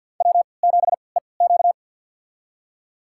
Morse code, heard as a keyed tone, about 50 wpm, spaced as for 28 wpm, 700 Hz. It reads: W7EQ
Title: USA Call Signs - 4 Characters - 3x Character Spacing - 50wpm